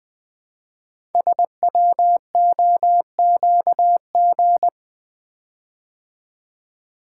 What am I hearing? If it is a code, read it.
Morse code, heard as a keyed tone, about 20 wpm, 700 Hz. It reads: SWOQG
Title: Code Groups - 5 Letters - 20wpm